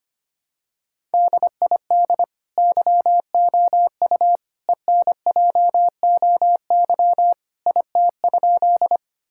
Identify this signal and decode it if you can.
Morse code, heard as a keyed tone, about 25 wpm, 700 Hz. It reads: DID YOU ENJOY IT?
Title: Sentences from Top 1000 Words - Review Part 5 - 25wpm